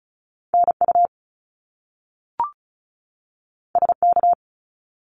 Morse code, heard as a keyed tone, about 36 wpm, 700 Hz. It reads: NU SK
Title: Canadian Provinces and Territories 35wpm